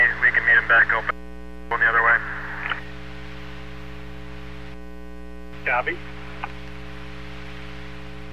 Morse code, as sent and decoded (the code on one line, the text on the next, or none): none